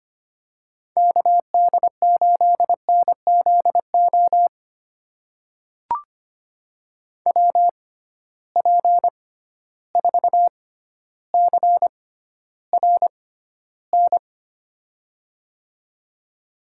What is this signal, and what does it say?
Morse code, heard as a keyed tone, about 25 wpm, spaced as for 13 wpm, 700 Hz. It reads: KD8NZO WP4CRN
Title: USA Call Signs - 6 Characters - 6x Character Spacing - 25wpm